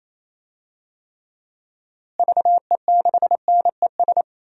Morse code, heard as a keyed tone, about 28 wpm, 700 Hz. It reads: VE6NES
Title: DXCC-100 Call Signs - 6 Characters - Repeated 3x - 28wpm